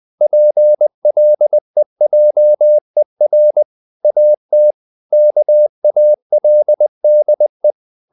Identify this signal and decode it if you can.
Morse code, heard as a keyed tone, about 20 wpm, 600 Hz. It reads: PLEJER AT KALDE
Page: Ferie 20 wpm